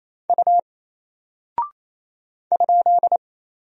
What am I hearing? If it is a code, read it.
Morse code, heard as a keyed tone, about 27 wpm, 700 Hz. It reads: U ?
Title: Single Character (Letter:Number:Punctuation:Prosign) 28wpm